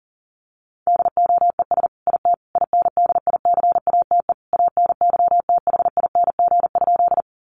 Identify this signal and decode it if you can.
Morse code, heard as a keyed tone, about 40 wpm, 700 Hz. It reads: DOES IT INDICATE ANYTHING?
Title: Sentences from Top 1300 Words 40wpm